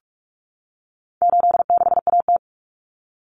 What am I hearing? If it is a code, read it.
Morse code, heard as a keyed tone, about 44 wpm, 700 Hz. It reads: 86AT
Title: Common Cut Numbers - 4-Digit 45wpm